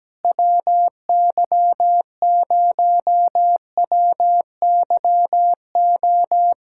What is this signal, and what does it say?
Morse code, heard as a keyed tone, about 17 wpm, 700 Hz. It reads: WY0WYO
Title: SOTA QSO Practice 17wpm